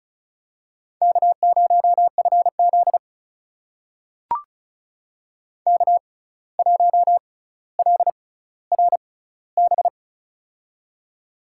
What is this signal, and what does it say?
Morse code, heard as a keyed tone, about 36 wpm, spaced as for 17 wpm, 700 Hz. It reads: K0FZ K1LRB
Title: USA Call Signs - 6x Character Spacing - 35wpm